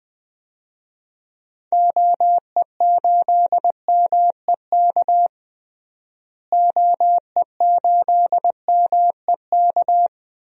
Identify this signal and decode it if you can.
Morse code, heard as a keyed tone, about 20 wpm, 700 Hz. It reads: OE8MEK OE8MEK
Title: DXCC-100 Call Signs - 6 Characters - Repeated 2x - 20wpm